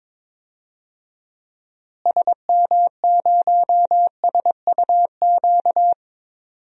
Morse code, heard as a keyed tone, about 22 wpm, 700 Hz. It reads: SM0SUQ
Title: DXCC-100 Call Signs - 6 Characters - Repeated 3x - 22wpm